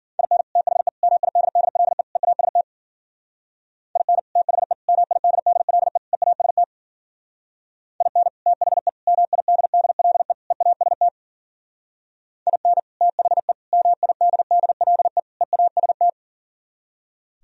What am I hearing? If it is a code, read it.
Morse code, heard as a keyed tone, about 48 wpm, 700 Hz. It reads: IN THE MIDDLE EAST IN THE MIDDLE EAST IN THE MIDDLE EAST IN THE MIDDLE EAST
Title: Speed-Racing - Sets of 4 Words 40wpm